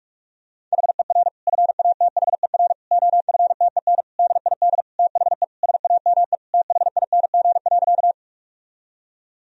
Morse code, heard as a keyed tone, about 42 wpm, 700 Hz. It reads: HER FATHER OFTEN DID THE SAME THING.
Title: Sentences from Top 1000 Words - Review Part 1 - 45wpm